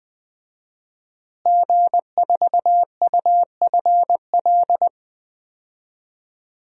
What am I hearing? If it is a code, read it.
Morse code, heard as a keyed tone, about 20 wpm, 700 Hz. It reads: G4UFL
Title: DXCC-100 Call Signs - 5 Characters - Repeated 3x - 20wpm